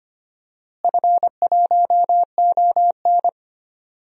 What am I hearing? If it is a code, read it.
Morse code, heard as a keyed tone, about 25 wpm, 700 Hz. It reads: F1ON
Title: DXCC-100 Call Signs - 4 Characters - Repeated 3x - 25wpm